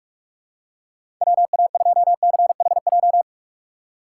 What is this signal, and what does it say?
Morse code, heard as a keyed tone, about 46 wpm, 700 Hz. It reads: W A 2 C H J
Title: USA Call Signs - 6 Characters - 6x Character Spacing - 45wpm